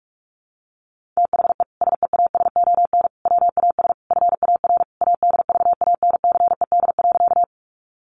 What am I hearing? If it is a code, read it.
Morse code, heard as a keyed tone, about 44 wpm, 700 Hz. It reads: THE SEASON WAS FAR ADVANCED.
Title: Sentences from Top 1400 Words 45wpm